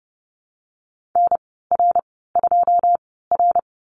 Morse code, heard as a keyed tone, about 30 wpm, spaced as for 17 wpm, 700 Hz. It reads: NR2R
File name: Lesson 14g - Number 2 - Call Signs 30wpm